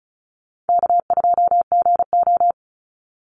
Morse code, heard as a keyed tone, about 34 wpm, 700 Hz. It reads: K2GO
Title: Call Signs Repeated 3x - 35wpm